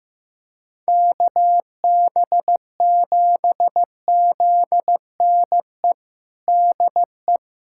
Morse code, heard as a keyed tone, about 16 wpm, 700 Hz. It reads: KB7ZNE DE
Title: USA Callsign DE Callsign 15wpm